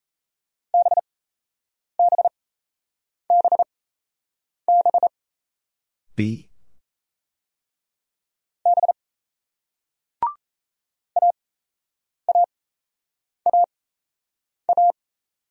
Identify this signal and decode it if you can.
Morse code, heard as a keyed tone, about 37 wpm, spaced as for 8 wpm, 700 Hz. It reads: BBBB B AAAA